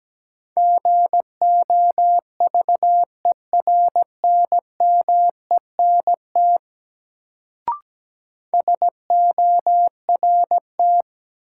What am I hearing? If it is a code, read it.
Morse code, heard as a keyed tone, about 17 wpm, 700 Hz. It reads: GOVERNMENT SORT
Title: Sets of 2 Words - New Word Drill - 17wpm